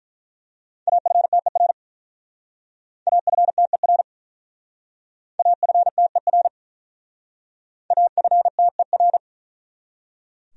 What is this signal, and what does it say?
Morse code, heard as a keyed tone, about 43 wpm, 700 Hz. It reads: AFTER AFTER AFTER AFTER
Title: Speed-Racing Top 500 Words 35wpm